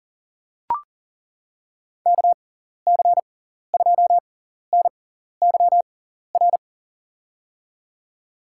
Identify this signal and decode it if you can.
Morse code, heard as a keyed tone, about 40 wpm, spaced as for 14 wpm, 700 Hz. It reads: KC2NYR
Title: USA Call Signs - 6 Characters - 6x Character Spacing - 40wpm